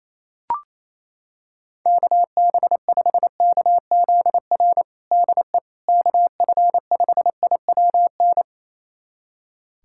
Morse code, heard as a keyed tone, about 28 wpm, 700 Hz. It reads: KB5KZR DE KF5IWN
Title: USA Callsign DE Callsign - 6 Characters - 28wpm